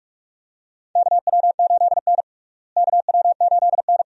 Morse code, heard as a keyed tone, about 43 wpm, 700 Hz. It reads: KW8N KW8N
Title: USA Call Signs - 4 Characters - Repeated 2x - 45wpm